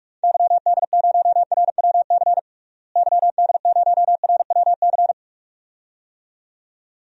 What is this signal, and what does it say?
Morse code, heard as a keyed tone, about 46 wpm, 700 Hz. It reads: YD0RWC YD0RWC
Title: DXCC-100 Call Signs - 6 Characters - Repeated 2x - 45wpm